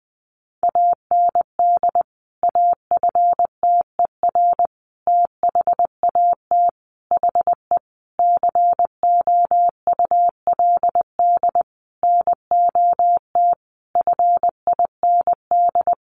Morse code, heard as a keyed tone, about 20 wpm, 700 Hz. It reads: AND AFTER THAT HE COULD NOT FIND
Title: Sentences from Top 1400 Words 20wpm